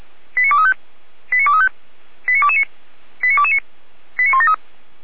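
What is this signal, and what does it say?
Five-tone selective call (ZVEI), heard as quick runs of five tones, tones of about 70 ms. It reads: 89235 89235 89229 89229 89173
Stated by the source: ZVEI five-tone call received by SDR, archived by Feuerwehr Orlamünde.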